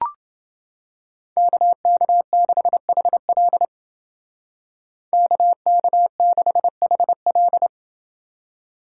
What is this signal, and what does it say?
Morse code, heard as a keyed tone, about 28 wpm, 700 Hz. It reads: KK6HL KK6HL